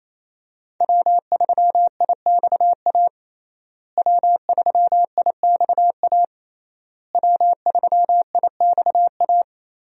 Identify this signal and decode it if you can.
Morse code, heard as a keyed tone, about 28 wpm, 700 Hz. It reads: W3IXA W3IXA W3IXA
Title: USA Call Signs - 5 Characters - Repeated 3x - 28wpm